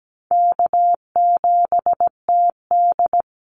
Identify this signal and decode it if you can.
Morse code, heard as a keyed tone, about 17 wpm, 700 Hz. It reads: K7TD
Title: Call Signs - 17wpm